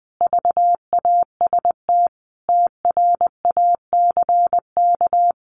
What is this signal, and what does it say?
Morse code, heard as a keyed tone, about 20 wpm, 700 Hz. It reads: VAST TRACK
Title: Sentences from Top 1300 Words 20wpm